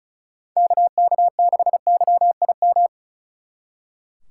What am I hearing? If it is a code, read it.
Morse code, heard as a keyed tone, about 45 wpm, 700 Hz. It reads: KK6YIM